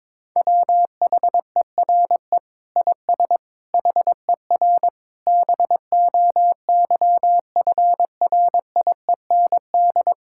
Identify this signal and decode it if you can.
Morse code, heard as a keyed tone, about 22 wpm, 700 Hz. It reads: WHERE IS HER BOYFRIEND
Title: Sentences from Top 1000 Words - Review Part 5 - 22wpm